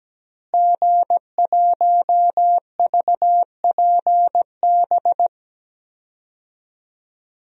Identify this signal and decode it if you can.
Morse code, heard as a keyed tone, about 17 wpm, 700 Hz. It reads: G1VPB
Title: DXCC-100 Call Signs - 5 Characters - Repeated 3x - 17wpm